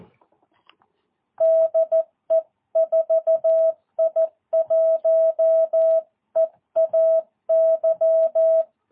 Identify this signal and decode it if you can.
Morse code, heard as a keyed tone, about 14 wpm, 650 Hz. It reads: DE 4I1EAY
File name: jh8cbh / 2022cw